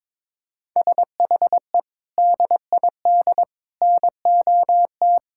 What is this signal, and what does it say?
Morse code, heard as a keyed tone, about 22 wpm, 700 Hz. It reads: SHE DID NOT
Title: Sentences from Top 1000 Words - Review Part 5 - 22wpm